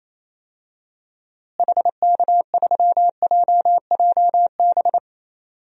Morse code, heard as a keyed tone, about 28 wpm, 700 Hz. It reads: HK3JJB
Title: DXCC-100 Call Signs - 6 Characters - Repeated 2x - 28wpm